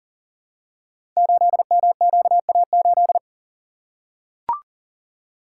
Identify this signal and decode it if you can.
Morse code, heard as a keyed tone, about 38 wpm, 700 Hz. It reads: 8MQA8
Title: Code Groups - 5 Letter:Numbers - 40wpm